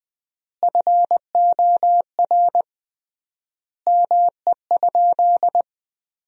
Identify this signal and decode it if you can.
Morse code, heard as a keyed tone, about 20 wpm, 700 Hz. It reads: FOR ME?
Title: 3x Word-Spacing - Sentences from Top 200 Words - 20wpm